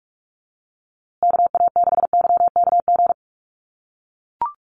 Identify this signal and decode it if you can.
Morse code, heard as a keyed tone, about 44 wpm, 700 Hz. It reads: KA6YKG
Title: USA Call Signs Repeated 2x 45wpm